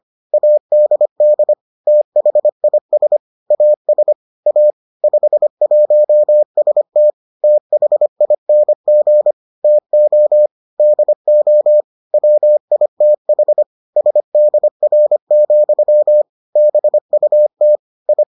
Morse code, heard as a keyed tone, about 25 wpm, 600 Hz. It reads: ADD THIS AS A 51ST THING TO DO WITH SDR, BUT I